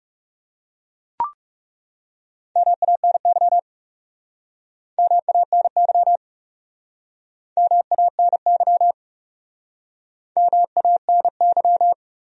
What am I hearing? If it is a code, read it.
Morse code, heard as a keyed tone, about 40 wpm, 700 Hz. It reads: MANY MANY MANY MANY